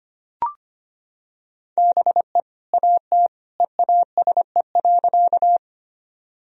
Morse code, heard as a keyed tone, about 25 wpm, 700 Hz. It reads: BE AT EASE.
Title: Sentences from Top 1000 Words - Review Part 1 - 25wpm